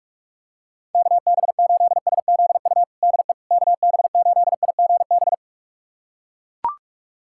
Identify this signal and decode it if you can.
Morse code, heard as a keyed tone, about 41 wpm, 690 Hz. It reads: KB8SZU DE KB8IGB
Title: USA Callsign DE Callsign - 6 Characters - 45wpm